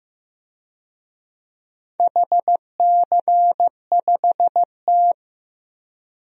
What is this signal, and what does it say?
Morse code, heard as a keyed tone, about 15 wpm, 700 Hz. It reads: HC5T
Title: DXCC-100 Call Signs - 4 Characters - 15wpm